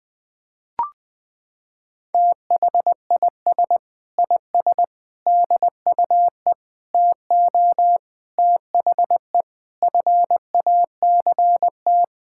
Morse code, heard as a keyed tone, about 21 wpm, 700 Hz. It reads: THIS IS DUE TO THE FACT
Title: Sets of 8 Words - 20wpm